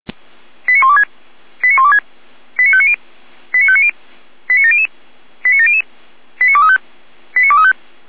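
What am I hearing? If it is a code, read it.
Five-tone selective call (ZVEI), heard as quick runs of five tones, tones of about 70 ms. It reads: 89126 89126 89590 89590 89700 89700 89235 89235
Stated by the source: ZVEI five-tone selective call 89094, received on SDR.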